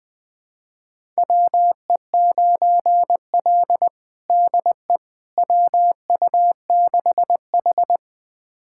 Morse code, heard as a keyed tone, about 20 wpm, 700 Hz. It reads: WE9L DE WU6H
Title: USA Callsign DE Callsign - 4 Characters - 20wpm